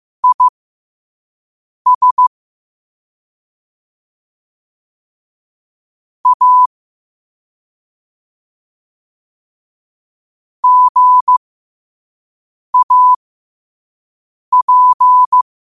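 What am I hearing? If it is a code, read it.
Morse code, heard as a keyed tone, about 15 wpm, spaced as for 5 wpm, 1000 Hz. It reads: IS A GAP
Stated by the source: Morse code sent at 5 wpm.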